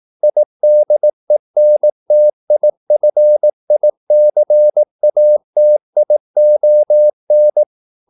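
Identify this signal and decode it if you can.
Morse code, heard as a keyed tone, about 18 wpm, 600 Hz. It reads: IDENTIFICATION
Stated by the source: Morse practice at 18 wpm, English text, from CW med Gnister.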